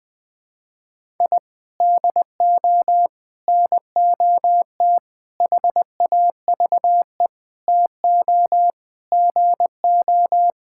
Morse code, heard as a keyed tone, about 21 wpm, 700 Hz. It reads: I DO NOT HAVE TO GO
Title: Sentences with 5-Word Phrases - 20wpm